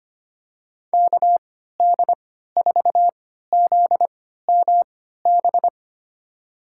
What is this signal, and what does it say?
Morse code, heard as a keyed tone, about 25 wpm, spaced as for 14 wpm, 700 Hz. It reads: KD4ZMB
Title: USA Call Signs - 3x Character Spacing - 25wpm